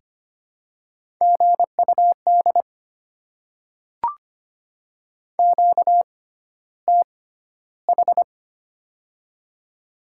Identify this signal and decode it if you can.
Morse code, heard as a keyed tone, about 25 wpm, spaced as for 13 wpm, 700 Hz. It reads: GUD QTH
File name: Top 20 QSO Elements - 6x Character Spacing - 25wpm